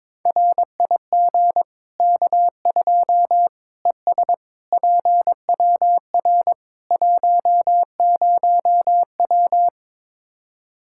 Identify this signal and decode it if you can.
Morse code, heard as a keyed tone, about 22 wpm, 700 Hz. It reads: RIG K2 ES PWR 10W
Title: Practice QSO - Exchange 2 - Info - 22wpm